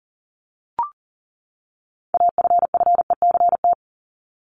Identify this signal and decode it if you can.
Morse code, heard as a keyed tone, about 40 wpm, 700 Hz. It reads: AFFECT
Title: Top 751-1000 Words 40wpm